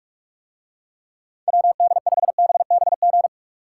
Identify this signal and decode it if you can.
Morse code, heard as a keyed tone, about 49 wpm, 700 Hz. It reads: WD5BBG